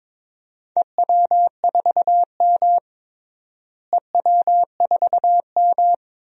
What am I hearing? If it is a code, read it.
Morse code, heard as a keyed tone, about 22 wpm, 700 Hz. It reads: EW4M EW4M
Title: DXCC-100 Call Signs - 4 Characters - Repeated 3x - 22wpm